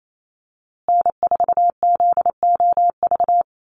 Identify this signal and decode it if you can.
Morse code, heard as a keyed tone, about 28 wpm, 700 Hz. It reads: N4ZOV
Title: USA Call Signs Repeated 3x 28wpm